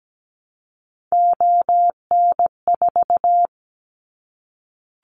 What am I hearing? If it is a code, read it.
Morse code, heard as a keyed tone, about 17 wpm, 700 Hz. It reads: ON4
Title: Callsign Prefixes for Top 25 DXCC - 17wpm